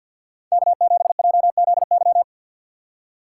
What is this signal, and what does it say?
Morse code, heard as a keyed tone, about 51 wpm, 700 Hz. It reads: K7JZY
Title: USA Call Signs - 5 Characters - 3x Character Spacing - 50wpm